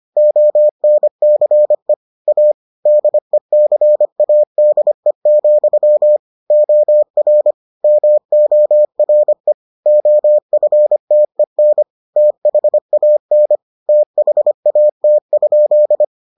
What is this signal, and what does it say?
Morse code, heard as a keyed tone, about 25 wpm, 600 Hz. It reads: ONCE A DECADE, OR MORE OFTEN THAN THAT?